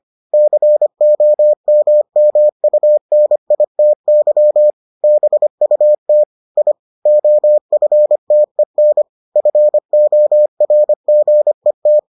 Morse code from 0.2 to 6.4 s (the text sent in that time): COMMUNITY BUT